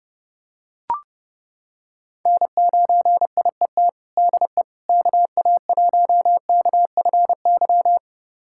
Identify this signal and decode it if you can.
Morse code, heard as a keyed tone, about 30 wpm, 700 Hz. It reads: N9IET DE KA1KFY
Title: USA Callsign DE Callsign 30wpm